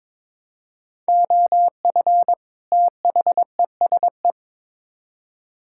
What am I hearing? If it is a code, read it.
Morse code, heard as a keyed tone, about 22 wpm, 700 Hz. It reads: OF THESE
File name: Build A Phrase - 2-Word Phrases - 22wpm